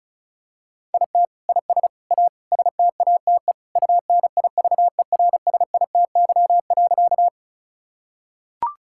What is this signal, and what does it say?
Morse code, heard as a keyed tone, about 36 wpm, 700 Hz. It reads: IT IS A STATE UNIVERSITY.
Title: Sentences from Top 1000 Words - Review Part 5 - 35wpm